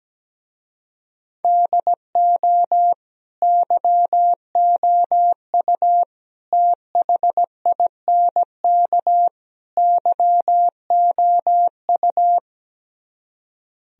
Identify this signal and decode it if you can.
Morse code, heard as a keyed tone, about 17 wpm, 700 Hz. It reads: DO YOU THINK YOU
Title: Speed-Racing - Sets of 4 Words 17wpm